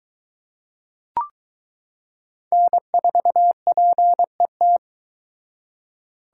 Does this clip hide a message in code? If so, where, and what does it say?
Yes: Morse code from 2.5 to 4.8 s, heard as a keyed tone, about 23 wpm, 700 Hz. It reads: N4PET